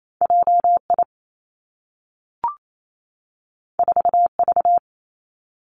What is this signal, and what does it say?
Morse code, heard as a keyed tone, about 28 wpm, 700 Hz. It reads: J I 4 V
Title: Two Challenging Letter-Number Combinations 28wpm